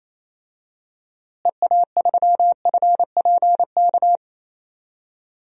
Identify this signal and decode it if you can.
Morse code, heard as a keyed tone, about 28 wpm, 700 Hz. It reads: EA3FPK
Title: DXCC-100 Call Signs - 6 Characters - Repeated 2x - 28wpm